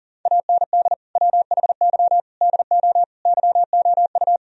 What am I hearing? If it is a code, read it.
Morse code, heard as a keyed tone, about 40 wpm, 700 Hz. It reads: AND WHY DO YOU
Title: Sentences from Top 1000 Words - Review Part 4 - 40wpm